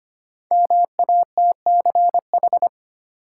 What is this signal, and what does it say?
Morse code, heard as a keyed tone, about 25 wpm, 700 Hz. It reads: MATCH
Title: Sentences from Top 1600 Words - New Word Drill - 25wpm